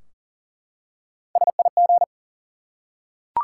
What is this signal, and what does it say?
Morse code, heard as a keyed tone, about 40 wpm, 700 Hz. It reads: SIG
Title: Top 90 QSO Elements 40wpm